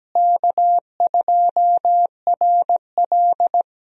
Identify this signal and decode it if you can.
Morse code, heard as a keyed tone, about 17 wpm, 700 Hz. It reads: K2RL
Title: USA Call Signs - 4 Characters - Repeated 2x - 17wpm